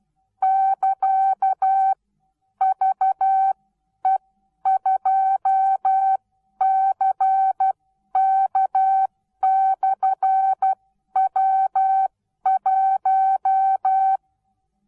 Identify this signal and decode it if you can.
Morse code, heard as a keyed tone, about 12 wpm, spaced as for 10 wpm, 770 Hz. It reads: <KA>VE2CK/W1
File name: Onlinelehrgang / morsen / DJ4UF Morsen19